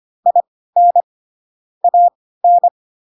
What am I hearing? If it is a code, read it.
Morse code, heard as a keyed tone, about 25 wpm, spaced as for 16 wpm, 700 Hz. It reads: IN AN
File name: Pinocchio / Pinocchio ch7 10 25 ik1hge